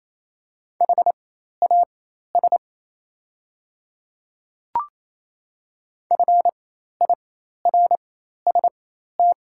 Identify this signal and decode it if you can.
Morse code, heard as a keyed tone, about 28 wpm, spaced as for 13 wpm, 700 Hz. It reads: HAS FIRST
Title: 4x Character Spacing - Top 100 Words - 28wpm